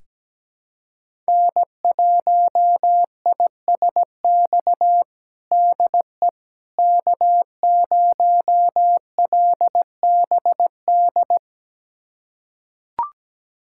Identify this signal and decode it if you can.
Morse code, heard as a keyed tone, about 17 wpm, 700 Hz. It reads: N1ISX DE K0LBD
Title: USA Callsign DE Callsign - 5 Characters - 17wpm